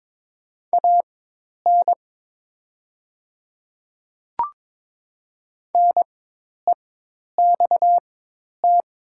Morse code, heard as a keyed tone, about 22 wpm, spaced as for 10 wpm, 700 Hz. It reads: AN NEXT
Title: 4x Character Spacing - Top 300 Words - 22wpm